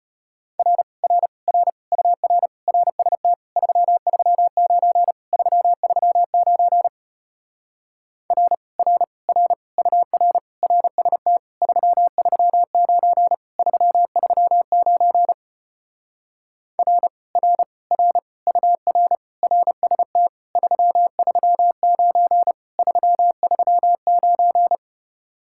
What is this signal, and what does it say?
Morse code, heard as a keyed tone, about 33 wpm, 700 Hz. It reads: R R R UR RST 339 339 R R R UR RST 339 339 R R R UR RST 339 339